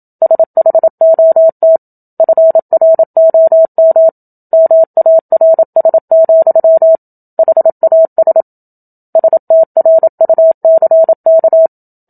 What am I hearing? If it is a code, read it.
Morse code, heard as a keyed tone, about 27 wpm, 650 Hz. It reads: SHOT FROM MARS, HAS STRUCK
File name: Book Two / 01-Chapter 1 - 27 WPM